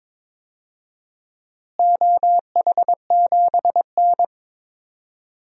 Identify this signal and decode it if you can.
Morse code, heard as a keyed tone, about 22 wpm, 700 Hz. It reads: OH7N
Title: DXCC-100 Call Signs - 4 Characters - Repeated 3x - 22wpm